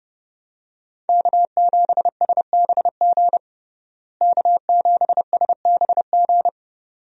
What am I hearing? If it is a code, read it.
Morse code, heard as a keyed tone, about 30 wpm, 700 Hz. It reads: K7SBG K7SBG
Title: USA Call Signs - 5 Characters - Repeated 2x - 30wpm